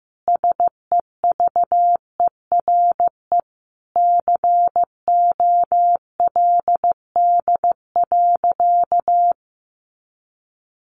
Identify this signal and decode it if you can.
Morse code, heard as a keyed tone, about 15 wpm, 700 Hz. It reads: SEVERE COLD.
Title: Sentences from Top 1400 Words 15wpm